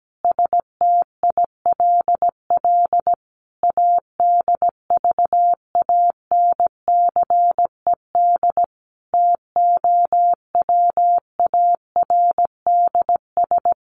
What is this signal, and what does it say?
Morse code, heard as a keyed tone, about 17 wpm, 700 Hz. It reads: STILL ADVANCED TOWARDS
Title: Sentences from Top 1200 Words 17wpm